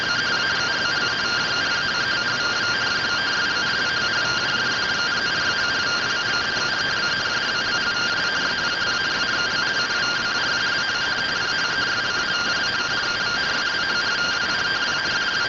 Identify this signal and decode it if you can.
Two-tone teleprinter signal, about 170 Hz shift.